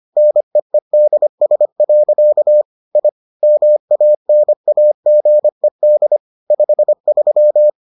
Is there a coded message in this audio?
Yes: Morse code from 0.2 to 7.7 s, heard as a keyed tone, about 25 wpm, 600 Hz. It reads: NEEDS. I MANAGED 53